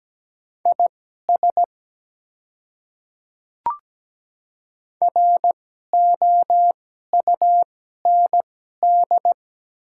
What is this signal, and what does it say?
Morse code, heard as a keyed tone, about 17 wpm, spaced as for 12 wpm, 690 Hz. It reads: IS ROUND
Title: 2x Character Spacing - Top 300 Words - 17wpm